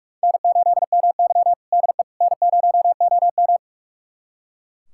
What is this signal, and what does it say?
Morse code, heard as a keyed tone, about 42 wpm, 700 Hz. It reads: N8MY DE N0OM
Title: USA Callsign DE Callsign - 4 Characters - 45wpm